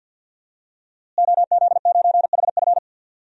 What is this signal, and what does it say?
Morse code, heard as a keyed tone, about 51 wpm, 700 Hz. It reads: OZ9HF
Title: DXCC-100 Call Signs - 5 Characters - Repeated 3x - 50wpm